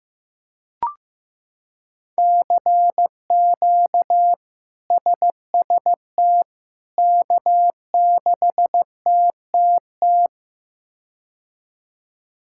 Morse code, heard as a keyed tone, about 15 wpm, 700 Hz. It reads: CQ SST K6TTT
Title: K1USN SST QSO Practice - 15wpm